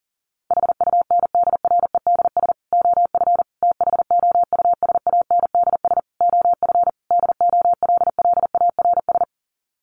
Morse code, heard as a keyed tone, about 40 wpm, 700 Hz. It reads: HUNDREDS OF THOUSANDS OF DOLLARS
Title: Sets of 5 Words Encore 40wpm